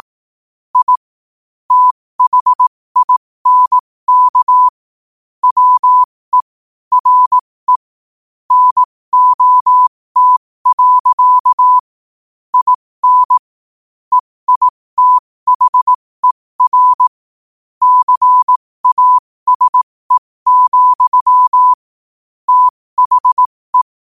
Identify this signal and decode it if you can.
Morse code, heard as a keyed tone, about 18 wpm, spaced as for 15 wpm, 1000 Hz. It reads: I THINK WE RE NOT. IN EITHER CASE, THE
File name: qotdfiles / 15wpm